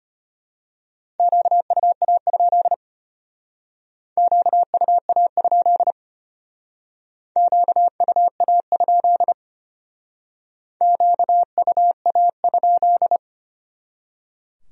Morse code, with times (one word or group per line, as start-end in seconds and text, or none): 1.2-2.7 QUA?
4.2-5.9 QUA?
7.4-9.3 QUA?
10.8-13.2 QUA?